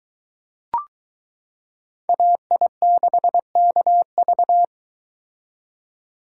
Morse code, heard as a keyed tone, about 22 wpm, 700 Hz. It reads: AI6KV